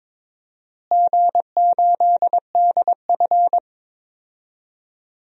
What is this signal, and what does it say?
Morse code, heard as a keyed tone, about 22 wpm, 700 Hz. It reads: G8DF
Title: DXCC-100 Call Signs - 4 Characters - Repeated 3x - 22wpm